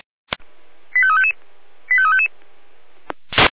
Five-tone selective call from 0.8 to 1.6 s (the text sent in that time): 85355